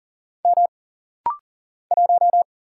Morse code, heard as a keyed tone, about 40 wpm, 700 Hz. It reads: M 1
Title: Single Letter-Number Alt - Rapid-Fire 40wpm